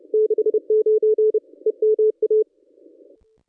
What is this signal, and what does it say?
Morse code, heard as a keyed tone, about 26 wpm, 440 Hz. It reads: 69 EMA
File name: contests / ss-cw-k5tr-op-n5rz -1100